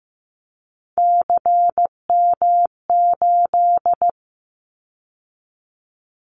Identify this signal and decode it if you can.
Morse code, heard as a keyed tone, about 15 wpm, 700 Hz. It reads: CM8
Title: Callsign Prefixes for Top 100 DXCC - 15wpm